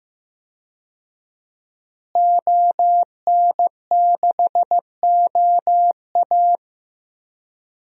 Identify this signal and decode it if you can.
Morse code, heard as a keyed tone, about 16 wpm, 700 Hz. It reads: ON6OA
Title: DXCC-100 Call Signs - 5 Characters - Repeated 3x - 15wpm